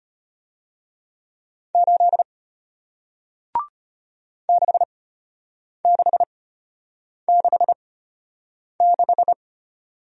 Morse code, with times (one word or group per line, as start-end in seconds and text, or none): 1.8-2.2 8
4.5-9.3 6666